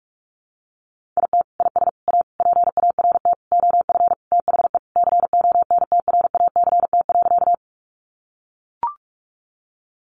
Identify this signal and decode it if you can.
Morse code, heard as a keyed tone, about 44 wpm, 700 Hz. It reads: IT IS A PART OF THE CONTRACT.
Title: Sentences from Top 1400 Words 45wpm